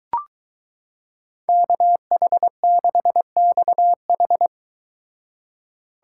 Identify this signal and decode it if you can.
Morse code, heard as a keyed tone, about 22 wpm, 700 Hz. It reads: KH6XH